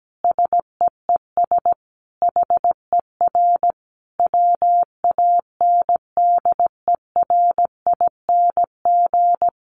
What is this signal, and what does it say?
Morse code, heard as a keyed tone, about 17 wpm, 700 Hz. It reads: SEES HER WANDERING
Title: Additional Easy Phrases 17wpm